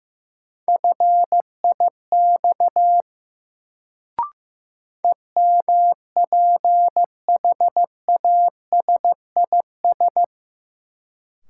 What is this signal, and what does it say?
Morse code, heard as a keyed tone, about 15 wpm, 700 Hz. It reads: FIX EMPHASIS
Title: Sentences from Top 1400 Words - New Word Drill - 15wpm